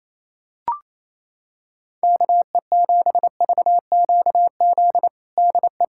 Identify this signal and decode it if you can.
Morse code, heard as a keyed tone, about 28 wpm, 700 Hz. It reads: KE7VQZ DE
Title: USA Callsign DE Callsign - 6 Characters - 28wpm